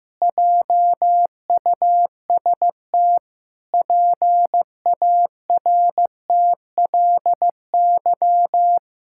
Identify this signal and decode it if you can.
Morse code, heard as a keyed tone, about 15 wpm, 700 Hz. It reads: JUST PARTLY
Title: Pangram Challenge v2 - 15wpm